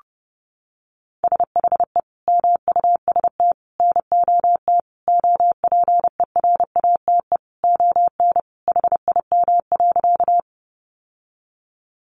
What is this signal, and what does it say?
Morse code, heard as a keyed tone, about 30 wpm, 700 Hz. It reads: SHE MUST NOT OPERATE ON HIM.
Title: Sentences from Top 1500 Words 30wpm